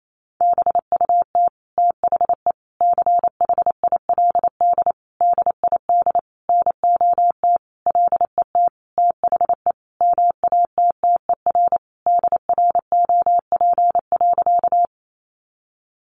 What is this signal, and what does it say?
Morse code, heard as a keyed tone, about 28 wpm, 700 Hz. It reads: BUT THE CHILD DID NOT LET THE MATTER DROP.
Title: Sentences from Top 1400 Words 28wpm